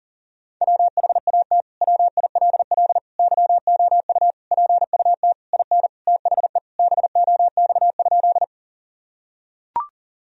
Morse code, heard as a keyed tone, about 40 wpm, 700 Hz. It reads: WHAT WILL YOU PUT IN THE BOX?
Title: Sentences from Top 1000 Words - Review Part 2 - 40wpm